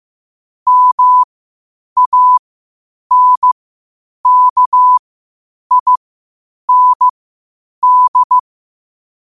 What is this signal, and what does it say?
Morse code, heard as a keyed tone, about 15 wpm, spaced as for 8 wpm, 1000 Hz. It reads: MANKIND
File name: qotdfiles / 8wpm